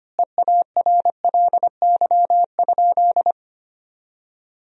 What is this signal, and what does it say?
Morse code, heard as a keyed tone, about 25 wpm, 700 Hz. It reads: EARLY?
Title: 2x Word-Spacing - Sentences from Top 400 Words - 25wpm